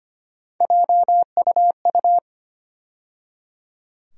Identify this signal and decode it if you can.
Morse code, heard as a keyed tone, about 25 wpm, 700 Hz. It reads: JUU
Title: USA Call Sign Postfixes - 3 Letters 25wpm